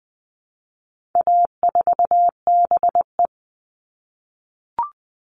Morse code, heard as a keyed tone, about 20 wpm, 700 Hz. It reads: A4BE
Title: All Cut Numbers - 4-Digit 20wpm